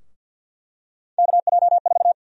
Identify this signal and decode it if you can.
Morse code, heard as a keyed tone, about 51 wpm, 700 Hz. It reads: KJ4
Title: USA Call Sign Prefixes - Repeated 2x - 50wpm